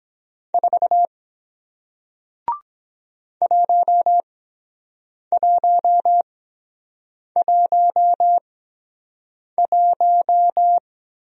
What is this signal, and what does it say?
Morse code, heard as a keyed tone, about 23 wpm, spaced as for 7 wpm, 700 Hz. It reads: 4 1111